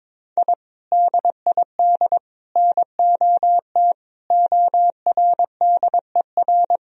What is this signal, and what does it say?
Morse code, heard as a keyed tone, about 22 wpm, 700 Hz. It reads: I DID NOT ORDER
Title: Sentences from Top 1000 Words - Review Part 2 - 22wpm